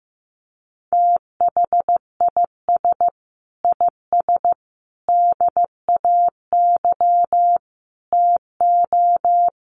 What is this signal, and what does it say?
Morse code, heard as a keyed tone, about 15 wpm, 700 Hz. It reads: THIS IS DAY TO
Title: Sets of 5 Words Encore 15wpm